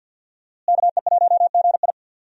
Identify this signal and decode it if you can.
Morse code, heard as a keyed tone, about 51 wpm, 700 Hz. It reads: KE1GI
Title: USA Call Signs - 5 Characters - Repeated 2x - 50wpm